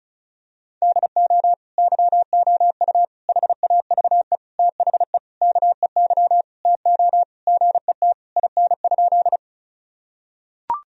Morse code, heard as a keyed tone, about 36 wpm, 700 Hz. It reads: DO YOU HAVE THE KEY TO GET IN?
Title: Sentences from Top 1000 Words - Review Part 4 - 35wpm